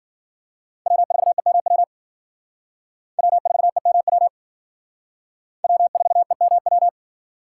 Morse code, heard as a keyed tone, about 50 wpm, 700 Hz. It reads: W4EMW W4EMW W4EMW
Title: Speed Racing - USA 1x3 Call Signs 40wpm